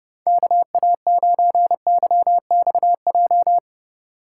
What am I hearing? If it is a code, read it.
Morse code, heard as a keyed tone, about 29 wpm, 700 Hz. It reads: KA9YXJ